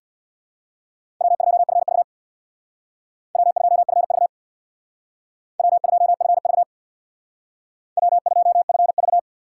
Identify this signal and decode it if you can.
Morse code, heard as a keyed tone, about 64 wpm, 700 Hz. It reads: W2FV W2FV W2FV W2FV